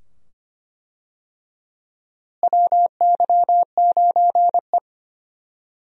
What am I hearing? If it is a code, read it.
Morse code, heard as a keyed tone, about 25 wpm, 700 Hz. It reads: WY9E
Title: USA Call Signs - 4 Characters - Repeated 2x - 25wpm